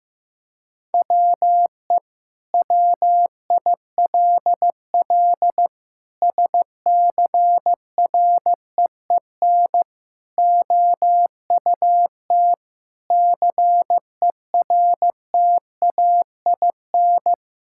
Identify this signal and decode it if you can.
Morse code, heard as a keyed tone, about 16 wpm, 700 Hz. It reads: WE WILL SCREEN OUT CERTAIN
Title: Sentences from Top 1000 Words - Review Part 6 - 15wpm